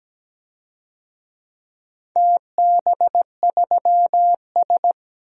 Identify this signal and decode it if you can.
Morse code, heard as a keyed tone, about 17 wpm, 700 Hz. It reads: TB3S
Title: DXCC-100 Call Signs - 4 Characters - 17wpm